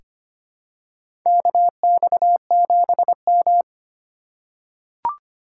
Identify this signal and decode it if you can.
Morse code, heard as a keyed tone, about 25 wpm, 700 Hz. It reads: KX7M
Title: K1USN SST QSO Practice - 25wpm